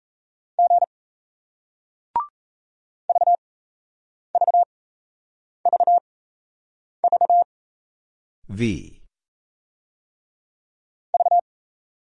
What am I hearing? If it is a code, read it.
Morse code, heard as a keyed tone, about 38 wpm, spaced as for 8 wpm, 700 Hz. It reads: G VVVV V